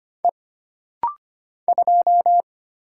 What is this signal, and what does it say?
Morse code, heard as a keyed tone, about 25 wpm, 700 Hz. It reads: E 2
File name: Single Character (Letter:Number:Punctuation:Prosign) - Rapid-Fire - 25wpm